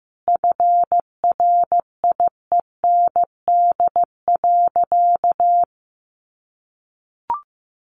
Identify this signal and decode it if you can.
Morse code, heard as a keyed tone, about 15 wpm, 700 Hz. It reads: FRIEND.
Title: Sentences from Top 1100 Words 15wpm